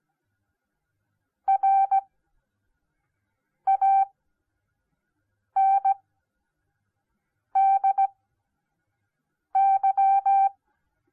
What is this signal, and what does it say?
Morse code, heard as a keyed tone, about 17 wpm, spaced as for 5 wpm, 780 Hz. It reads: RANDY